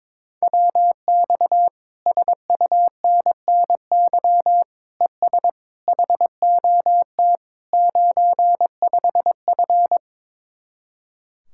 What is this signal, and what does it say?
Morse code, heard as a keyed tone, about 22 wpm, 700 Hz. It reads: WX SUNNY ES HOT 95F
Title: Speed-Racing Common QSO Phrases - 22wpm